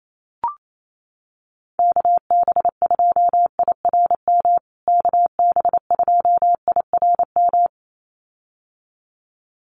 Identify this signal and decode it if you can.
Morse code, heard as a keyed tone, about 28 wpm, 700 Hz. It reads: KB2IRM KB2IRM
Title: USA Call Signs Repeated 2x 28wpm